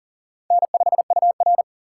Morse code, heard as a keyed tone, about 39 wpm, 690 Hz. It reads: N5UR
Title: USA Call Signs - 4 Characters - 2x Character Spacing - 40wpm